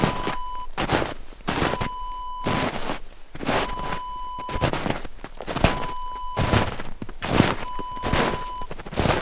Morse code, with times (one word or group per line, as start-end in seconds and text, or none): none